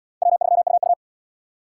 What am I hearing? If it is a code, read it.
Morse code, heard as a keyed tone, about 69 wpm, 700 Hz. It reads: W3UU